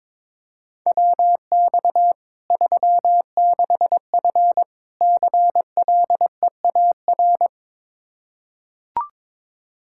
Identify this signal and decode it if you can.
Morse code, heard as a keyed tone, about 22 wpm, 700 Hz. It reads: WX 36F CLEAR
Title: Practice QSO - Exchange 2 - Info - 22wpm